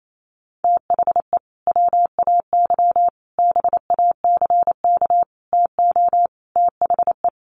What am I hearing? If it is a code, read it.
Morse code, heard as a keyed tone, about 28 wpm, 700 Hz. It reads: THE WAY BACK TO THE
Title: Sets of 5 Words Encore2 28wpm